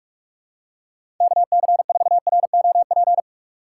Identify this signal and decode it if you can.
Morse code, heard as a keyed tone, about 42 wpm, 700 Hz. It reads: KC4ROP
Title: USA Call Signs - 2x Character Spacing - 45wpm